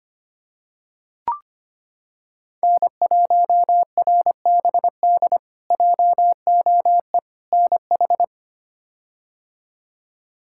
Morse code, heard as a keyed tone, about 25 wpm, 700 Hz. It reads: N1RBD JOE NH
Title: K1USN SST QSO Practice - 25wpm